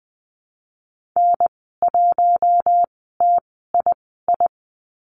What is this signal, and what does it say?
Morse code, heard as a keyed tone, about 20 wpm, spaced as for 15 wpm, 700 Hz. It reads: N1TII
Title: 076 Lesson 13g - Letter L - Call Signs 20wpm